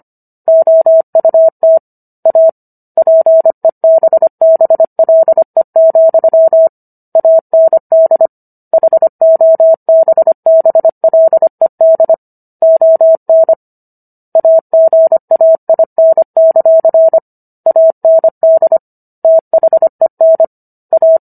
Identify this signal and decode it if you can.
Morse code, heard as a keyed tone, about 25 wpm, 650 Hz. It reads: OUT A PEBBLE, AND HOBBLED ON AGAIN; AND THEN A